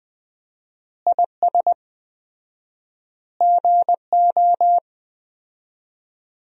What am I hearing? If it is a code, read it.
Morse code, heard as a keyed tone, about 20 wpm, 700 Hz. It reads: IS GO
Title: 4x Word-Spacing - Sentences from Top 200 Words - 20wpm